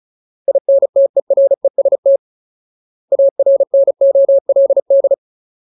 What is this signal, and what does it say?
Morse code, heard as a keyed tone, about 34 wpm, 550 Hz. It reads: INTEREST ARNOLD
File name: Ultimate Word List / 35WPM Ultimate